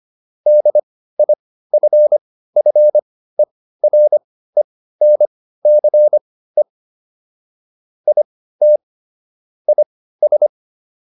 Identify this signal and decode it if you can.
Morse code, heard as a keyed tone, about 25 wpm, spaced as for 15 wpm, 600 Hz. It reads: DIFFERENCE IT IS